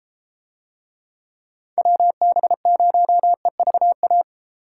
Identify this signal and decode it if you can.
Morse code, heard as a keyed tone, about 31 wpm, 700 Hz. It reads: WB0EVA